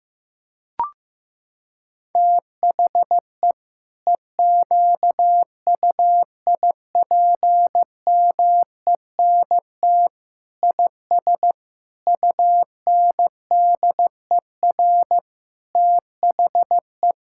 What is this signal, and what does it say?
Morse code, heard as a keyed tone, about 15 wpm, 700 Hz. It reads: THE EQUIPMENT IS UNDER THE
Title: Sentences from Top 1000 Words - Review Part 5 - 15wpm